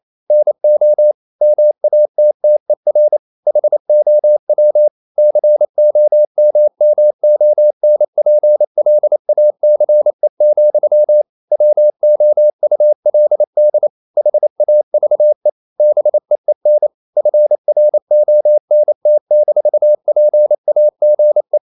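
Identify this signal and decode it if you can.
Morse code, heard as a keyed tone, about 28 wpm, 600 Hz. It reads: NO MATTER HOW COMMONPLACE, WOULD HAVE BEEN FRONT-PAGE
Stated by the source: Morse code practice at 28 wpm, text of Ken Follett's novel Never in English, from CW med Gnister.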